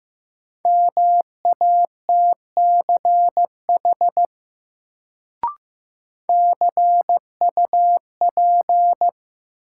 Morse code, heard as a keyed tone, about 15 wpm, 700 Hz. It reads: MATCH CUP
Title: Sentences from Top 1600 Words - New Word Drill - 15wpm